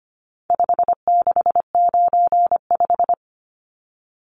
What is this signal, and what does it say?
Morse code, heard as a keyed tone, about 25 wpm, 700 Hz. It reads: 5695
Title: Numbers - 4-Digit 25wpm